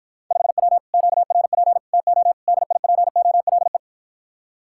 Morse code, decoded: 5W QRP TO DIPOLE

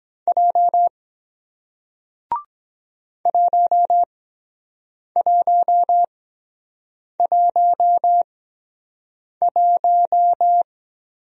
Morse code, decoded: J 1111

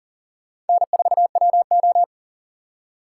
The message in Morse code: N4WO

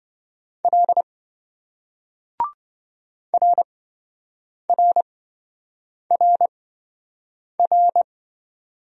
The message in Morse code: L RRRR

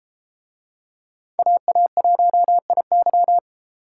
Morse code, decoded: AA1IY

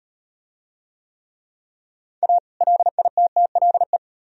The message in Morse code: A LITTLE

